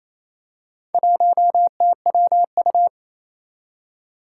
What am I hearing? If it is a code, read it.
Morse code, heard as a keyed tone, about 27 wpm, 700 Hz. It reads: 1TWU
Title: Four Challenging Letter-Number Combinations 28wpm